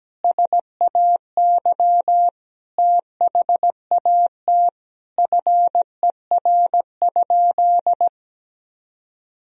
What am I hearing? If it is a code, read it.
Morse code, heard as a keyed tone, about 17 wpm, 700 Hz. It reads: SAY THAT FER?
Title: Sentences from Top 200 Words and Abbreviations 17wpm